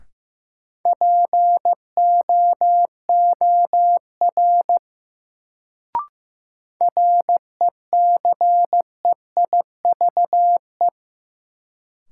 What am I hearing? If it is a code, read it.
Morse code, heard as a keyed tone, about 15 wpm, 700 Hz. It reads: POOR RECEIVE
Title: Sentences from Top 1000 Words - New Word Drill - 15wpm